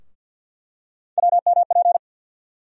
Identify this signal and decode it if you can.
Morse code, heard as a keyed tone, about 52 wpm, 690 Hz. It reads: WMP
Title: USA Call Sign Postfixes - 3 Letters 50wpm